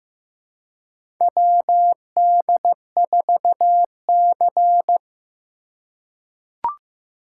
Morse code, heard as a keyed tone, about 15 wpm, 700 Hz. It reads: WD4C
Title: USA Call Signs - 4 Characters - 2x Character Spacing - 15wpm